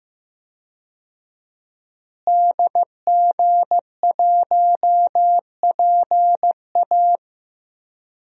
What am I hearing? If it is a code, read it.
Morse code, heard as a keyed tone, about 15 wpm, 700 Hz. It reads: DG1PA
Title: DXCC-100 Call Signs - 5 Characters - Repeated 2x - 15wpm